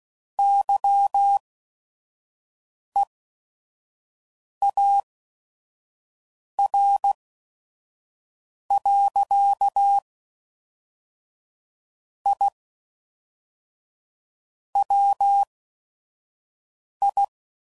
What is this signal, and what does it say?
Morse code, heard as a keyed tone, about 16 wpm, spaced as for 5 wpm, 800 Hz. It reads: YEAR. I WI
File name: CW Code Groups @5WPM / QSOs / QSO WD0FBT